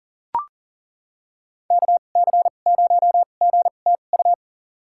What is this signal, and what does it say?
Morse code, heard as a keyed tone, about 40 wpm, spaced as for 29 wpm, 700 Hz. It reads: KC0GTU
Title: USA Call Signs - 6 Characters - 2x Character Spacing - 40wpm